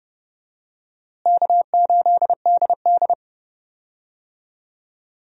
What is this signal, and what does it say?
Morse code, heard as a keyed tone, about 30 wpm, 700 Hz. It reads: K8DD
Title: USA Call Signs - 4 Characters - Repeated 3x - 30wpm